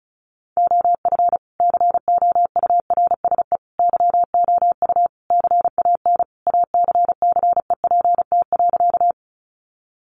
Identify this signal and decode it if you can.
Morse code, heard as a keyed tone, about 34 wpm, 700 Hz. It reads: OF COURSE YOU CAN ACCEPT.
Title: Sentences from Top 1300 Words 35wpm